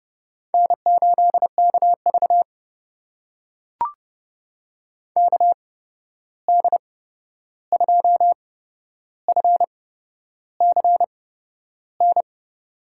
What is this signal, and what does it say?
Morse code, heard as a keyed tone, about 30 wpm, spaced as for 11 wpm, 700 Hz. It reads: N8KV KD2FCN
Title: USA Call Signs - 8x Character Spacing - 30wpm